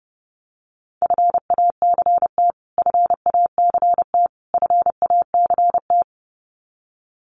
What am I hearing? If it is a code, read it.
Morse code, heard as a keyed tone, about 30 wpm, 700 Hz. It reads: FACT FACT FACT
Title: Top 500 Words Repeated 30wpm